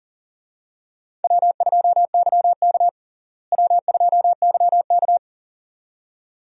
Morse code, heard as a keyed tone, about 40 wpm, 700 Hz. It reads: W2YK W2YK
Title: USA Call Signs - 4 Characters - Repeated 2x - 40wpm